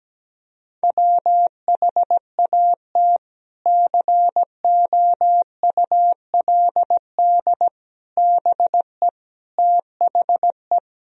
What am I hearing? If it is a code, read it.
Morse code, heard as a keyed tone, about 17 wpm, 700 Hz. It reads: WHAT COULD BE THE
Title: Sentences from Top 1000 Words - Review Part 3 - 17wpm